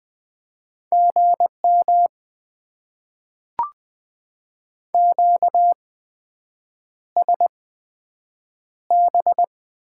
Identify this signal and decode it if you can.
Morse code, heard as a keyed tone, about 20 wpm, spaced as for 7 wpm, 700 Hz. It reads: GM QSB
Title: Top 90 QSO Elements - 8x Character Spacing - 20wpm